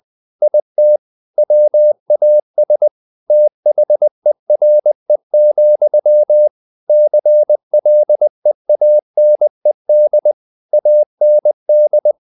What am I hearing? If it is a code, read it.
Morse code, heard as a keyed tone, about 20 wpm, 600 Hz. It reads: IT WAS THERE, CLEANED AND
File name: the old man and the sea part 2 20wpm